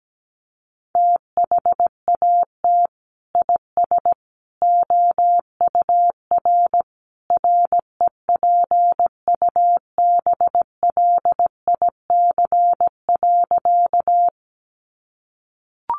Morse code, heard as a keyed tone, about 17 wpm, 700 Hz. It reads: THAT IS OUR REPUBLIC.
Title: Sentences from Top 1500 Words 17wpm